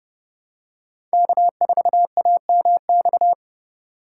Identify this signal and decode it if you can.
Morse code, heard as a keyed tone, about 30 wpm, 700 Hz. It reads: K4AMX
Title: USA Call Signs - 5 Characters - 3x Character Spacing - 30wpm